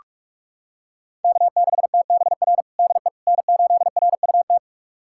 Morse code, heard as a keyed tone, about 46 wpm, 700 Hz. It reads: K6TBR DE N8RUT
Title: USA Callsign DE Callsign - 5 Characters - 45wpm